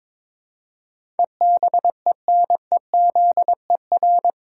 Morse code, heard as a keyed tone, about 22 wpm, 700 Hz. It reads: EBENEZER
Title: Pangram Challenge v2 - 22wpm